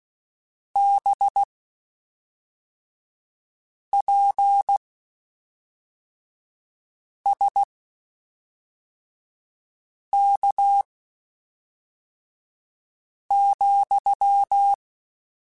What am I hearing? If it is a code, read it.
Morse code, heard as a keyed tone, about 16 wpm, spaced as for 3 wpm, 800 Hz. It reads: BPSK,
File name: (spaced) / QSOs Spaced-mp3s / kc7sac @5 sp